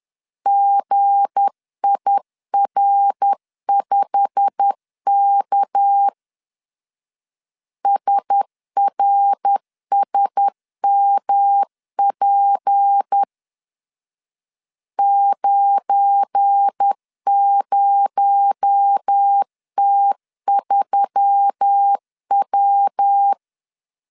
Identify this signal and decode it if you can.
Morse code, heard as a keyed tone, about 10 wpm, 800 Hz. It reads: GIR5K SRSMP 90T3W
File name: CW / DET / 1 9 1500C S A 10wpm 4511